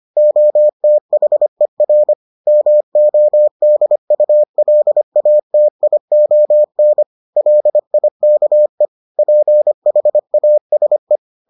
Morse code, heard as a keyed tone, about 25 wpm, 600 Hz. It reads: OTHER MODULATION LIKE PHASE